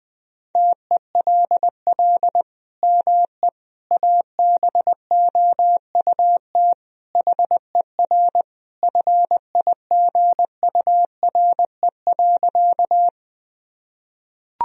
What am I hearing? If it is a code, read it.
Morse code, heard as a keyed tone, about 20 wpm, 700 Hz. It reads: TELL ME ABOUT HER FIGURE.
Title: Sentences from Top 1000 Words - Review Part 2 - 20wpm